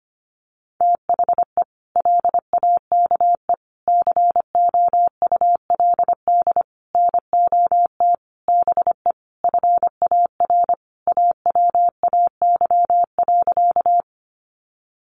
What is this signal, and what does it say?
Morse code, heard as a keyed tone, about 25 wpm, 700 Hz. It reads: THE LAKE COULD NOT BE FAR AWAY.
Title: Sentences from Top 1100 Words 25wpm